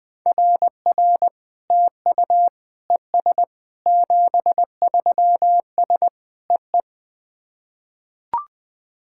Morse code, heard as a keyed tone, about 20 wpm, 700 Hz. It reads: RR TU ES 73S EE
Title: SOTA QSO Practice 20wpm